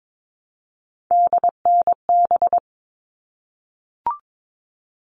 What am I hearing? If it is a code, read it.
Morse code, heard as a keyed tone, about 22 wpm, 700 Hz. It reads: DNB
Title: All Cut Numbers - 3-Digit 22wpm